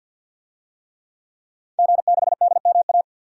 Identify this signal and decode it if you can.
Morse code, heard as a keyed tone, about 46 wpm, 700 Hz. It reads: G6DMA
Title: DXCC-100 Call Signs - 5 Characters - Repeated 3x - 50wpm